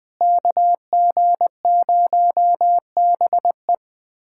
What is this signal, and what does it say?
Morse code, heard as a keyed tone, about 20 wpm, 700 Hz. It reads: KG0BE
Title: USA Callsign DE Callsign 20wpm